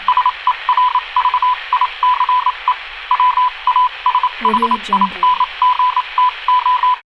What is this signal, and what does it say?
Morse code, heard as a keyed tone, about 27 wpm, spaced as for 24 wpm, 1000 Hz. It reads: SERVICE WASHINGTO